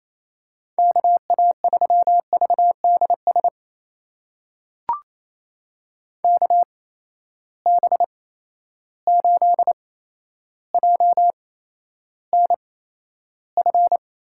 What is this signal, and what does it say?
Morse code, heard as a keyed tone, about 28 wpm, spaced as for 12 wpm, 700 Hz. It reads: KA3VDS KB8JNF
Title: USA Call Signs - 6 Characters - 8x Character Spacing - 28wpm